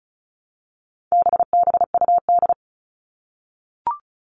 Morse code, heard as a keyed tone, about 34 wpm, 700 Hz. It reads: BBUD
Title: All Cut Numbers - 4-Digit 35wpm